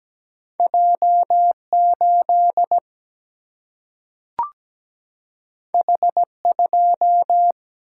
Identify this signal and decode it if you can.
Morse code, heard as a keyed tone, about 17 wpm, 700 Hz. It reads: J 8 H 2
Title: Single-Letter Single-Number 17wpm